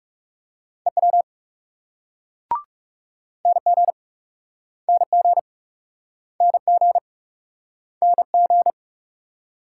Morse code, heard as a keyed tone, about 40 wpm, 700 Hz. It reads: EW NG NG NG NG